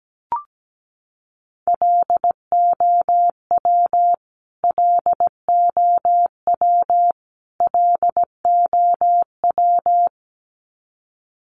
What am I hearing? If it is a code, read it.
Morse code, heard as a keyed tone, about 17 wpm, 700 Hz. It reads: LOW LOW LOW
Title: Top 200 Words Repeated 17wpm